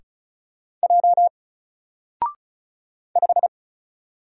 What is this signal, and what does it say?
Morse code, heard as a keyed tone, about 35 wpm, 700 Hz. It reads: J 5
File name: Single-Letter-Number-35wpm